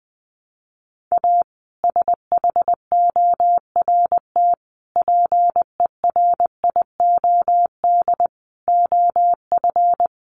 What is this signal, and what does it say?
Morse code, heard as a keyed tone, about 20 wpm, 700 Hz. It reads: A SHORT PERIOD OF
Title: Sets of 5 Words Encore 20wpm